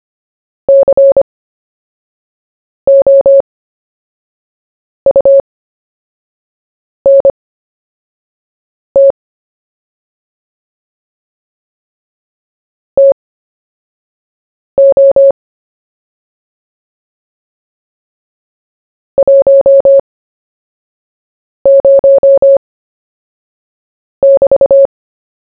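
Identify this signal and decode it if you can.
Morse code, heard as a keyed tone, about 25 wpm, spaced as for 5 wpm, 550 Hz. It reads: COUNT TO 10<BT>
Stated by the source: CW story-telling practice, Short Story 1 sent at 5 wpm, by VK4JE, Queensland, Australia.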